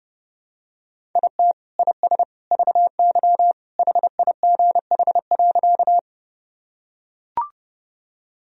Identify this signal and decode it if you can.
Morse code, heard as a keyed tone, about 32 wpm, 700 Hz. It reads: IT IS VY HIGH.
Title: Sentences from Top 200 Words and Abbreviations 30wpm